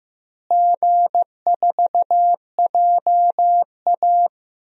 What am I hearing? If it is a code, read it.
Morse code, heard as a keyed tone, about 15 wpm, 700 Hz. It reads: G4JA
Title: DXCC-100 Call Signs - 4 Characters - Repeated 2x - 15wpm